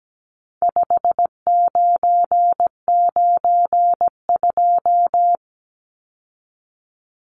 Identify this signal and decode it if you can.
Morse code, heard as a keyed tone, about 17 wpm, 700 Hz. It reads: 5992
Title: Numbers - 4-Digit 17wpm